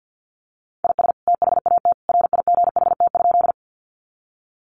Morse code, heard as a keyed tone, about 50 wpm, 700 Hz. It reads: IS THAT RIGHT?
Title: Sentences from Top 200 Words 50wpm